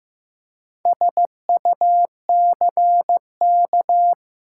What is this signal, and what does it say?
Morse code, heard as a keyed tone, about 15 wpm, 700 Hz. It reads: SUCK
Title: Four Letter Words - 15wpm